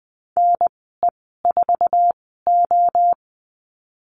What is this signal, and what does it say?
Morse code, heard as a keyed tone, about 20 wpm, spaced as for 14 wpm, 700 Hz. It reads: NE4O
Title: 048 Lesson 9g - Number 4 - Call Signs 20wpm